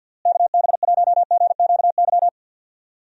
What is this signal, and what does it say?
Morse code, heard as a keyed tone, about 52 wpm, 700 Hz. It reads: KB1GQY
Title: USA Call Signs - 6 Characters - Repeated 3x - 50wpm